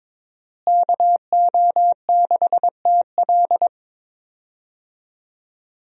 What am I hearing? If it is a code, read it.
Morse code, heard as a keyed tone, about 22 wpm, 700 Hz. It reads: KO6TL
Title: USA Call Signs - 5 Characters - Repeated 2x - 22wpm